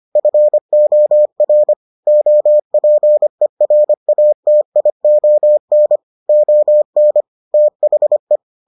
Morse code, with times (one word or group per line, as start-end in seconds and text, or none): 0.2-1.7 FOR
2.1-6.0 OPERATION
6.3-7.2 ON
7.5-8.4 THE